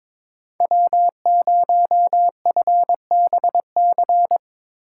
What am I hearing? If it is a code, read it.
Morse code, heard as a keyed tone, about 22 wpm, 700 Hz. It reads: W0FBC
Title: USA Call Signs - 5 Characters - 3x Character Spacing - 22wpm